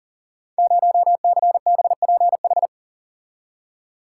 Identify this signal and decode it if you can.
Morse code, heard as a keyed tone, about 43 wpm, 700 Hz. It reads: 0CBPH
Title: Code Groups - 5 Letter:Numbers - 40wpm